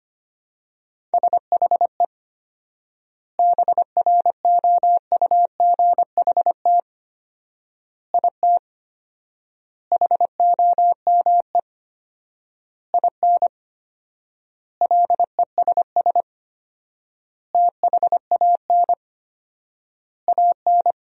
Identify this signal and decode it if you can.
Morse code, heard as a keyed tone, about 25 wpm, 700 Hz. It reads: SHE BROUGHT IT HOME IN LESS THAN AN
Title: 4x Word-Spacing - Sentences from Top 500 Words - 25wpm